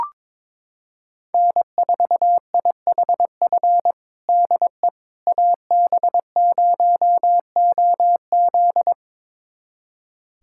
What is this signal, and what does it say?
Morse code, heard as a keyed tone, about 22 wpm, 700 Hz. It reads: N4IHF DE AB0OZ
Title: USA Callsign DE Callsign - 5 Characters - 22wpm